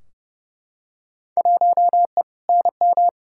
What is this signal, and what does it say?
Morse code, heard as a keyed tone, about 30 wpm, 700 Hz. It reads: 1E NM
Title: Field Day Exchanges - 30wpm